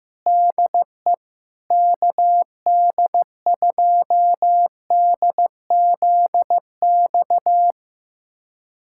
Morse code, heard as a keyed tone, about 15 wpm, 700 Hz. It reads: DE KD2DZX
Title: USA Callsign DE Callsign - 6 Characters - 15wpm